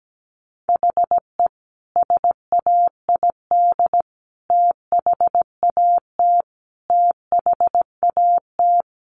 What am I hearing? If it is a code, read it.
Morse code, heard as a keyed tone, about 17 wpm, 700 Hz. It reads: HE SAID THAT THAT
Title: Sentences from Top 200 Words 17wpm